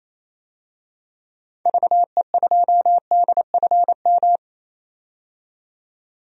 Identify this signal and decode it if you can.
Morse code, heard as a keyed tone, about 28 wpm, 700 Hz. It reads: VE2DFM
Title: DXCC-100 Call Signs - 6 Characters - 28wpm